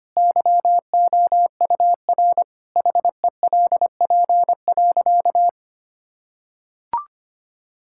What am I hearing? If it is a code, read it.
Morse code, heard as a keyed tone, about 25 wpm, 700 Hz. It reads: YOUR HELP.
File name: practice / 25wpm / Sentences with 4-Word Phrases - 25wpm